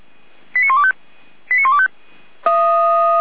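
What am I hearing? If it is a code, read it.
Five-tone selective call (ZVEI), heard as quick runs of five tones, tones of about 70 ms. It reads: 89125 89125
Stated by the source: ZVEI five-tone selective call received on an SDR.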